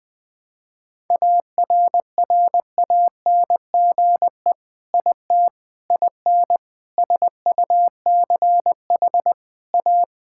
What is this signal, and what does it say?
Morse code, heard as a keyed tone, about 21 wpm, 700 Hz. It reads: ARRANGE IT IN SUCH A
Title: Sentences with 6-Word Phrases - 20wpm